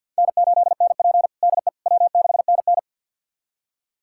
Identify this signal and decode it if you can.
Morse code, heard as a keyed tone, about 52 wpm, 700 Hz. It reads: N8NP DE W6NN
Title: USA Callsign DE Callsign - 4 Characters - 50wpm